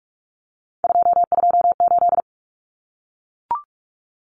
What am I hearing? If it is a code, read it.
Morse code, heard as a keyed tone, about 44 wpm, 700 Hz. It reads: 228
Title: Common Cut Numbers - 3-Digit 45wpm